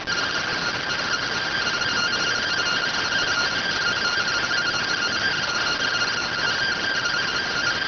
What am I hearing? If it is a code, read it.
Two-tone teleprinter signal, about 170 Hz shift.